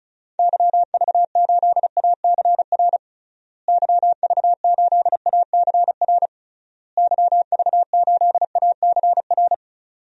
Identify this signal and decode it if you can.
Morse code, heard as a keyed tone, about 36 wpm, 700 Hz. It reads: YV8ACR YV8ACR YV8ACR
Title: DXCC-100 Call Signs - 6 Characters - Repeated 3x - 35wpm